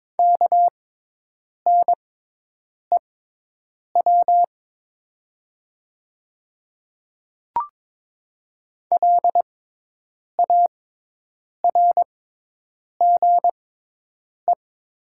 Morse code, heard as a keyed tone, about 22 wpm, spaced as for 8 wpm, 700 Hz. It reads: KNEW LARGE
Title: 6x Character Spacing - Top 500 Words - 22wpm